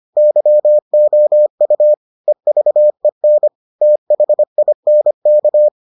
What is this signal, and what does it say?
Morse code, heard as a keyed tone, about 25 wpm, 600 Hz. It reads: YOU EVEN THINK